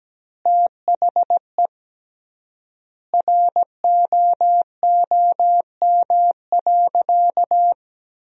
Morse code, heard as a keyed tone, about 17 wpm, 700 Hz. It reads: THE ROOM.
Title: 3x Word-Spacing - Sentences from Top 300 Words - 17wpm